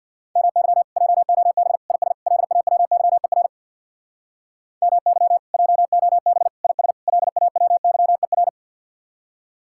Morse code, heard as a keyed tone, about 64 wpm, 700 Hz. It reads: MY JOB IS LAWYER MY JOB IS LAWYER